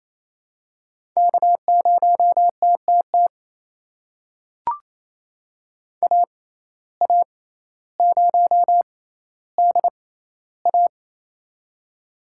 Morse code, heard as a keyed tone, about 29 wpm, spaced as for 15 wpm, 700 Hz. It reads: K0TTT AA0DA
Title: USA Call Signs - 5 Characters - 6x Character Spacing - 28wpm